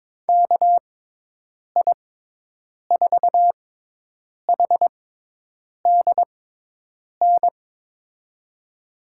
Morse code, heard as a keyed tone, about 22 wpm, spaced as for 8 wpm, 700 Hz. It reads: KI4HDN
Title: USA Call Signs - 6 Characters - 6x Character Spacing - 22wpm